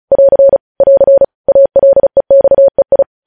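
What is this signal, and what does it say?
Morse code, heard as a keyed tone, about 36 wpm, 550 Hz. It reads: <AR> <AR> ALEXEI
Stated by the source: Morse news headlines sent at 35.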